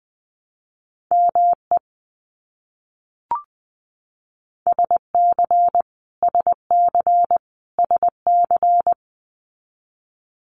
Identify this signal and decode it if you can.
Morse code, heard as a keyed tone, about 20 wpm, 700 Hz. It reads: ME SC SC SC
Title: US States Repeated 3x 20wpm